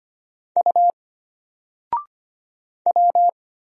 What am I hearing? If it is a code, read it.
Morse code, heard as a keyed tone, about 25 wpm, 700 Hz. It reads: U W
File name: Single-Letter-Number-25wpm